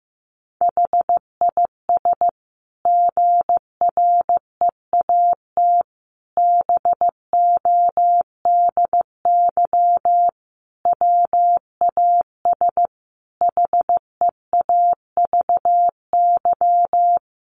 Morse code, heard as a keyed tone, about 15 wpm, 700 Hz. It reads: HIS GREAT BODY WAS HEAVY
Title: Sentences from Top 1400 Words 15wpm